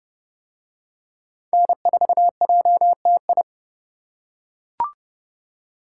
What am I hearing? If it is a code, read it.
Morse code, heard as a keyed tone, about 30 wpm, 700 Hz. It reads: N4JTI